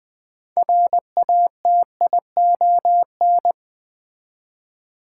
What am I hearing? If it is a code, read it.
Morse code, heard as a keyed tone, about 21 wpm, 700 Hz. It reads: RATION
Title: Build A Word - 3 Letters At A Time - 8-Letter Words - 20wpm